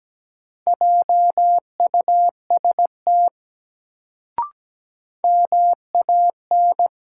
Morse code, heard as a keyed tone, about 17 wpm, 700 Hz. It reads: JUST MAN
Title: Sentences from Top 200 Words - New Word Drill - 17wpm